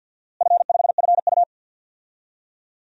Morse code, heard as a keyed tone, about 47 wpm, 700 Hz. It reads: F5FU
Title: DXCC-100 Call Signs - 4 Characters - Repeated 2x - 50wpm